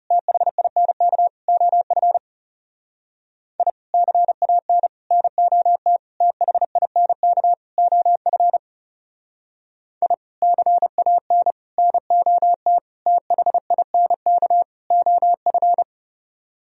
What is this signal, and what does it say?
Morse code, heard as a keyed tone, about 38 wpm, 700 Hz. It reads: THINK OF I CAN NOT THINK OF I CAN NOT THINK OF